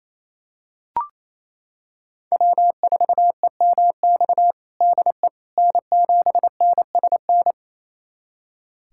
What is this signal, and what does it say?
Morse code, heard as a keyed tone, about 28 wpm, 700 Hz. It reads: W4EMX DE N7NSN
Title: USA Callsign DE Callsign - 5 Characters - 28wpm